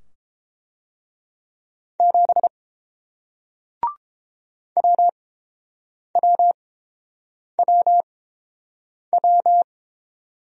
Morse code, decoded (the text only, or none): 7 WWWW